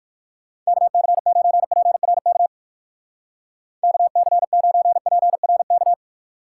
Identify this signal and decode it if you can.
Morse code, heard as a keyed tone, about 58 wpm, 700 Hz. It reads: KC9PRK KC9PRK